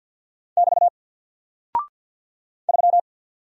Morse code, heard as a keyed tone, about 48 wpm, 700 Hz. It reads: <BT> 3
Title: Single Character (Letter:Number:Punctuation:Prosign) 50wpm